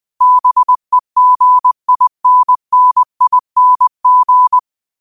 Morse code, heard as a keyed tone, about 20 wpm, 1000 Hz. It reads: BEGINNING